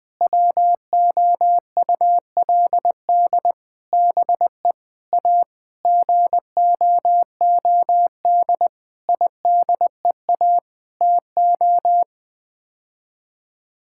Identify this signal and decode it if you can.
Morse code, heard as a keyed tone, about 20 wpm, 700 Hz. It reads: WOULD BE A GOOD IDEA TO
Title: Sets of 9 Words - 20wpm